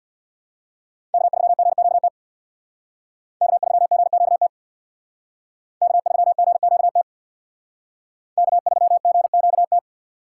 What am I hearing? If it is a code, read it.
Morse code, heard as a keyed tone, about 64 wpm, 700 Hz. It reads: K3GQT K3GQT K3GQT K3GQT